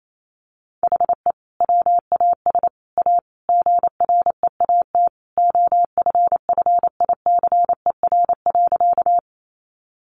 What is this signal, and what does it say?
Morse code, heard as a keyed tone, about 28 wpm, 700 Hz. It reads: HE WAS A GREAT OFFICER.
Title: Sentences from Top 1200 Words 28wpm